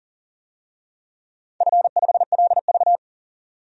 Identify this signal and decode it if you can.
Morse code, heard as a keyed tone, about 40 wpm, 690 Hz. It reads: F5LV
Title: DXCC-100 Call Signs - 4 Characters - 40wpm